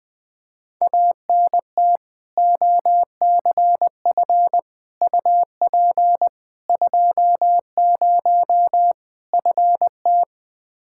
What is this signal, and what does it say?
Morse code, heard as a keyed tone, about 20 wpm, 700 Hz. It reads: ANT OCF UP 20 FT
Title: K1USN SST QSO Practice 20wpm